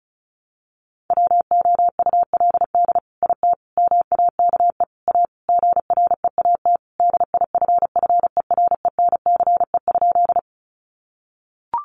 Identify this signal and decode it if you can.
Morse code, heard as a keyed tone, about 34 wpm, 700 Hz. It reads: WOULD IT MAKE A GREAT DIFFERENCE?
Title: Sentences from Top 800 Words 35wpm